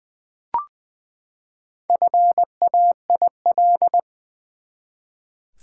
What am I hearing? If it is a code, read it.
Morse code, heard as a keyed tone, about 20 wpm, 700 Hz. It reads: FAIL
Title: Sentences from Top 1100 Words - New Word Drill Alt v2 - 20wpm